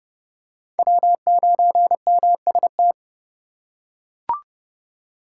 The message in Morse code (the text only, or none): W9MST